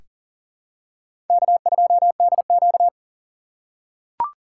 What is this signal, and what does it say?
Morse code, heard as a keyed tone, about 38 wpm, 700 Hz. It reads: K2DQ
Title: USA Call Signs - 4x Character Spacing - 40wpm